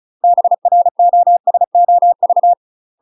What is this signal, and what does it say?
Morse code, heard as a keyed tone, about 34 wpm, 700 Hz. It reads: BROSOV
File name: UK MASTER QSOs TRAINING / 35WPM / UK MASTER QSOs 0005